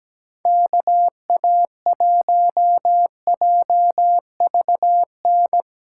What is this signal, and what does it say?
Morse code, heard as a keyed tone, about 17 wpm, 700 Hz. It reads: KA1JVN
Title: USA Call Signs - 6 Characters - 2x Character Spacing - 17wpm